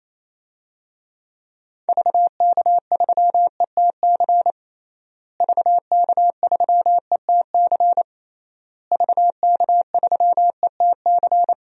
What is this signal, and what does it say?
Morse code, heard as a keyed tone, about 28 wpm, 700 Hz. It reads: VK3ETC VK3ETC VK3ETC
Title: DXCC-100 Call Signs - 6 Characters - Repeated 3x - 28wpm